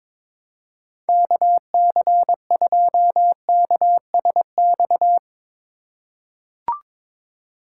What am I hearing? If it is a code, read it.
Morse code, heard as a keyed tone, about 22 wpm, 700 Hz. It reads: KC2KSX
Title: USA Call Signs - 6 Characters - 3x Character Spacing - 22wpm